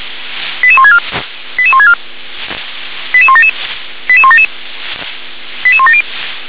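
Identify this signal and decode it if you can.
Five-tone selective call (ZVEI), heard as quick runs of five tones, tones of about 70 ms. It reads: 88164 88164 88177 88177 88170
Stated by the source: ZVEI five-tone selective call received with an SDR.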